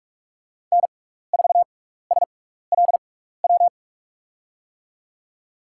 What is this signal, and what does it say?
Morse code, heard as a keyed tone, about 43 wpm, spaced as for 15 wpm, 700 Hz. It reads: N4SLW
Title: USA Call Signs - 5 Characters - 6x Character Spacing - 45wpm